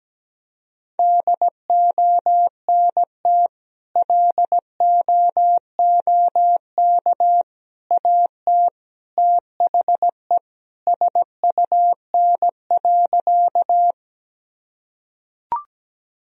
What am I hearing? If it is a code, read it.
Morse code, heard as a keyed tone, about 17 wpm, 700 Hz. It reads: DONT LOOK AT THE SUN.
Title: Sentences from Top 1000 Words - Review Part 1 - 17wpm